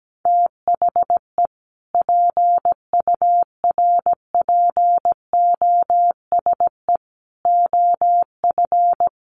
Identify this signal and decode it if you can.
Morse code, heard as a keyed tone, about 17 wpm, 700 Hz. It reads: THE PURPOSE OF
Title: Sentences from Top 800 Words 17wpm